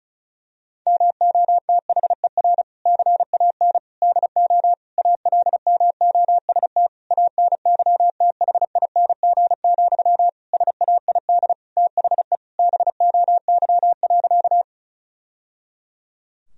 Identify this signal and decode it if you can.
Morse code, heard as a keyed tone, about 36 wpm, 700 Hz. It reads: MOTHER CAN DO ALMOST ANYTHING, SAID THE BOY.
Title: Sentences from Top 1000 Words - Review Part 4 - 35wpm